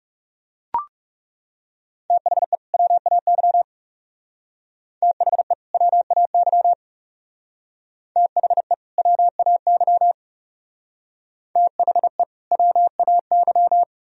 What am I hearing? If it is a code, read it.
Morse code, heard as a keyed tone, about 38 wpm, 700 Hz. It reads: THE WAY THE WAY THE WAY THE WAY